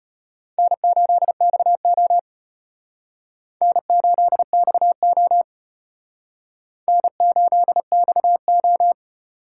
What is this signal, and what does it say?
Morse code, heard as a keyed tone, about 32 wpm, 700 Hz. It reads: N8XO N8XO N8XO